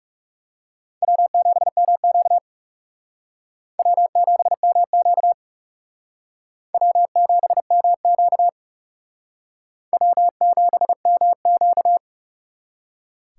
Morse code, heard as a keyed tone, about 37 wpm, 700 Hz. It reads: W7MQ W7MQ W7MQ W7MQ
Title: Speed Racing - USA 1x2 Call Signs 30wpm